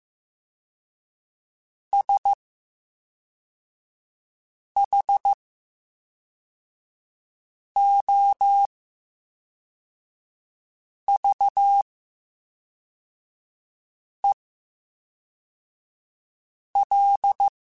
Morse code, heard as a keyed tone, about 15 wpm, spaced as for 3 wpm, 800 Hz. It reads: SHOVEL
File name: Special INGs / Special INGs-mp3s / VerySlowToFaster Ing @10